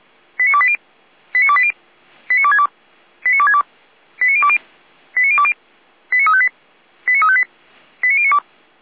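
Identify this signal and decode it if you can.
Five-tone selective call (ZVEI), heard as quick runs of five tones, tones of about 70 ms. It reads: 89280 89280 89262 89262 89020 89020 89358 89358 89092
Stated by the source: ZVEI five-tone call received by SDR, recorded by the Orlamünde fire brigade (feuerwehr-orlamuende).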